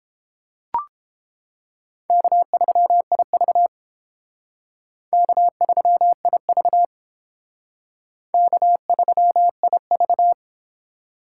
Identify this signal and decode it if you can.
Morse code, heard as a keyed tone, about 30 wpm, 700 Hz. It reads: K3IV K3IV K3IV